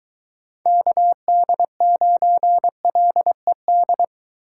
Morse code, heard as a keyed tone, about 22 wpm, 700 Hz. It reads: KD9LED